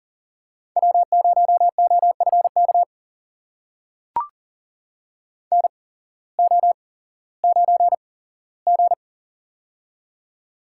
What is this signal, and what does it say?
Morse code, heard as a keyed tone, about 40 wpm, spaced as for 19 wpm, 700 Hz. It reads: W0OFK NO9G
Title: USA Call Signs - 8x Character Spacing - 40wpm